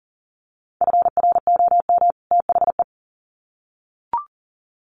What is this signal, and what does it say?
Morse code, heard as a keyed tone, about 40 wpm, 700 Hz. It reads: FROM THE
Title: Sets of 2 Words 40wpm